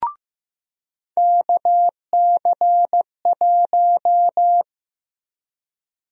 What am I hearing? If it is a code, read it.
Morse code, heard as a keyed tone, about 15 wpm, 700 Hz. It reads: KC1
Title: USA Call Sign Prefixes - 15wpm